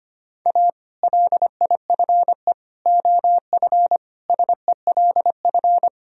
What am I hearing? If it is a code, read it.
Morse code, heard as a keyed tone, about 25 wpm, 700 Hz. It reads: A LIFE OF SELF
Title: Sentences from Top 1000 Words - Review Part 1 - 25wpm